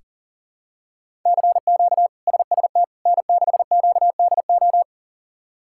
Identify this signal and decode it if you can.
Morse code, heard as a keyed tone, about 40 wpm, 700 Hz. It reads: CQ SST N6QDO
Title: K1USN SST QSO Practice - 40wpm